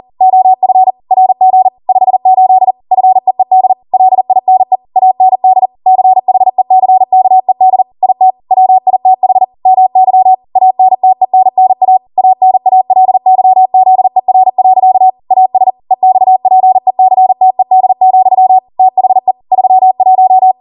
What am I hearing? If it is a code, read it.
Morse code, heard as a keyed tone, about 40 wpm, 750 Hz. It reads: OF RG 58 FEED LINE AND CHECKED IT WITH MY ANTENNA ANALYZER. AS EXPECTED, THE 31